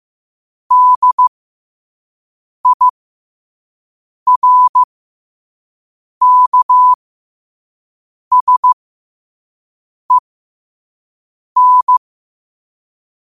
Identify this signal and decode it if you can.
Morse code, heard as a keyed tone, about 15 wpm, spaced as for 5 wpm, 1000 Hz. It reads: DIRKSEN
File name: qotdfiles / 5wpm